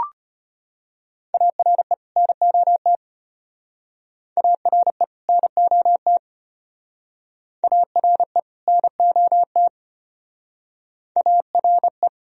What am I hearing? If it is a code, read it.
Morse code, heard as a keyed tone, about 34 wpm, 700 Hz. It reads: ARE NOT ARE NOT ARE NOT ARE